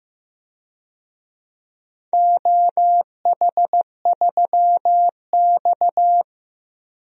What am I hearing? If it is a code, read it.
Morse code, heard as a keyed tone, about 15 wpm, 700 Hz. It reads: OH3X
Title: DXCC-100 Call Signs - 4 Characters - 15wpm